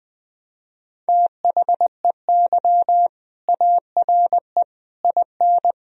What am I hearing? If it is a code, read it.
Morse code, heard as a keyed tone, about 21 wpm, 700 Hz. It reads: THEY ARE IN
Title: Sentences from Top 1000 Words - Review Part 3 - 20wpm